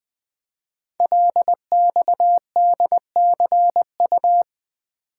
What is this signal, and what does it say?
Morse code, heard as a keyed tone, about 20 wpm, 700 Hz. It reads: LXDCU
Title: Code Groups - 5 Letters - 20wpm